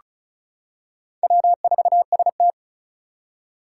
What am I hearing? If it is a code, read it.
Morse code, heard as a keyed tone, about 36 wpm, 700 Hz. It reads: W4ST
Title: Four Challenging Letter-Number Combinations 35wpm